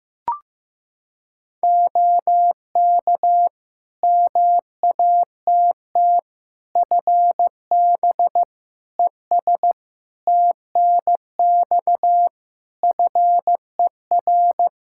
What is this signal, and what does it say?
Morse code, heard as a keyed tone, about 15 wpm, 700 Hz. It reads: OK MATT FB ES TNX FER
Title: Practice QSO - Exchange 2 - Info - 15wpm